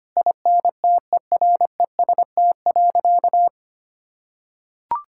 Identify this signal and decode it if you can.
Morse code, heard as a keyed tone, about 25 wpm, 700 Hz. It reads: INTEREST.
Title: Sentences from Top 1000 Words - Review Part 5 - 25wpm